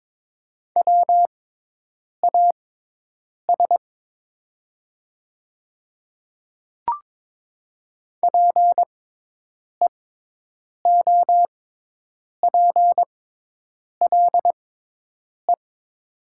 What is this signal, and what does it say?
Morse code, heard as a keyed tone, about 22 wpm, spaced as for 8 wpm, 700 Hz. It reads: WAS PEOPLE
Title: 6x Character Spacing - Top 100 Words - 22wpm